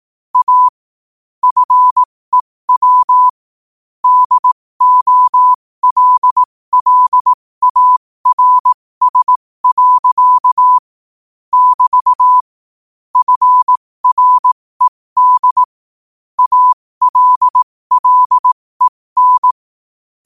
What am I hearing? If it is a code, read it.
Morse code, heard as a keyed tone, about 18 wpm, spaced as for 15 wpm, 1000 Hz. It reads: A FEW DOLLARS. <BT> FRED ALLEN
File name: qotdfiles / 15wpm